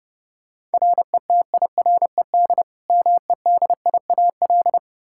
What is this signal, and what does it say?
Morse code, heard as a keyed tone, about 32 wpm, 700 Hz. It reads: RETIRED MEDIAL
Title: Practice QSO - Exchange 3 - More Info - 30wpm